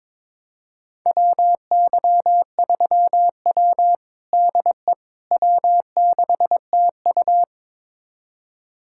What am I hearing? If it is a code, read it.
Morse code, heard as a keyed tone, about 22 wpm, 700 Hz. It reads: WY3W DE W6TU
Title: USA Callsign DE Callsign - 4 Characters - 22wpm